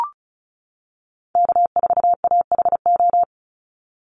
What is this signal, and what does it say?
Morse code, heard as a keyed tone, about 34 wpm, 700 Hz. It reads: K4AHO
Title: Call Signs - 35wpm